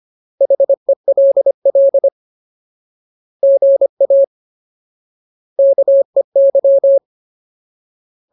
Morse code, heard as a keyed tone, about 25 wpm, 550 Hz. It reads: HELL GA KEY
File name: Ultimate Word List / 25WPM Ultimate